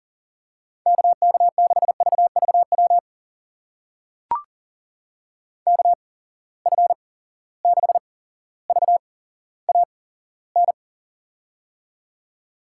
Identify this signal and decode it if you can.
Morse code, heard as a keyed tone, about 37 wpm, spaced as for 17 wpm, 700 Hz. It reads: KK6VVW KF6VAN
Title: USA Call Signs - 6 Characters - 8x Character Spacing - 40wpm